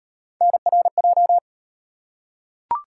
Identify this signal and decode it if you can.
Morse code, heard as a keyed tone, about 38 wpm, 700 Hz. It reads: NRJ